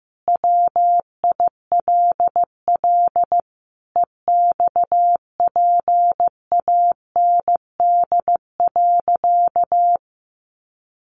Morse code, heard as a keyed tone, about 15 wpm, 700 Hz. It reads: WILL EXPAND.
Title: Sentences from Top 1400 Words 15wpm